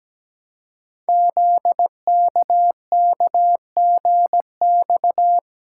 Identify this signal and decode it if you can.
Morse code, heard as a keyed tone, about 17 wpm, 700 Hz. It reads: ZKKGX
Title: Code Groups - 5 Letter:Numbers - 17wpm